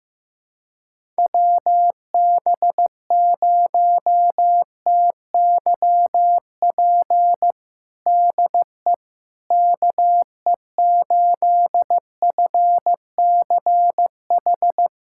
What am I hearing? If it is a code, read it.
Morse code, heard as a keyed tone, about 15 wpm, 700 Hz. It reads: WB0TYP DE KE8FCH
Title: USA Callsign DE Callsign - 6 Characters - 15wpm